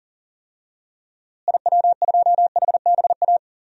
Morse code, heard as a keyed tone, about 40 wpm, 700 Hz. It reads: IW2HBA
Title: DXCC-100 Call Signs - 6 Characters - Repeated 3x - 40wpm